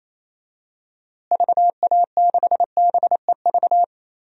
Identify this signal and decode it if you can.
Morse code, heard as a keyed tone, about 27 wpm, 700 Hz. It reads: VA6BEV
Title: DXCC-100 Call Signs - 6 Characters - Repeated 3x - 28wpm